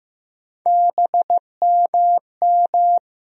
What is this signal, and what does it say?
Morse code, heard as a keyed tone, about 15 wpm, 700 Hz. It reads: BMM
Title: USA Call Sign Postfixes - 3 Letters 15wpm